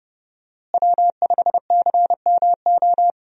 Morse code, heard as a keyed tone, about 29 wpm, 700 Hz. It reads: W5CMO
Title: USA Call Signs - 5 Characters - Repeated 2x - 30wpm